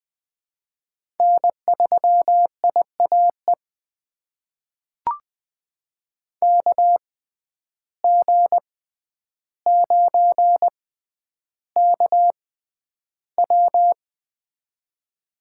Morse code, heard as a keyed tone, about 20 wpm, spaced as for 10 wpm, 700 Hz. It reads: N3IAE KG9KW
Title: USA Call Signs - 5 Characters - 6x Character Spacing - 20wpm